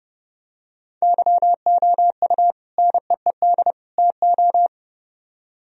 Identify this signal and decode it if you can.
Morse code, heard as a keyed tone, about 29 wpm, 700 Hz. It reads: YOU NEED TO